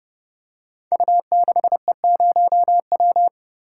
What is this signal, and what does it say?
Morse code, heard as a keyed tone, about 30 wpm, 700 Hz. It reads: U6E0W
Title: Code Groups - 5 Letter:Numbers - 30wpm